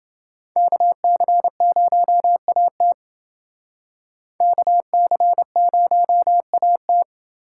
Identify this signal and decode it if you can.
Morse code, heard as a keyed tone, about 28 wpm, 700 Hz. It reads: KC0AT KC0AT